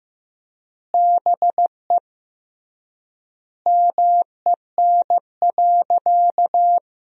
Morse code, heard as a keyed tone, about 15 wpm, 700 Hz. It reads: BE MEN.
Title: 3x Word-Spacing - Sentences from Top 300 Words - 15wpm